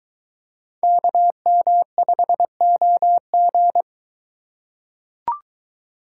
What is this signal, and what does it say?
Morse code, heard as a keyed tone, about 22 wpm, 700 Hz. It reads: KM5OG